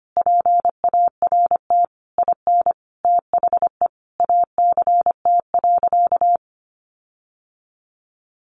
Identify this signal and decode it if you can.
Morse code, heard as a keyed tone, about 25 wpm, 700 Hz. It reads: PART IN THE ACT.
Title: Sentences from Top 1300 Words 25wpm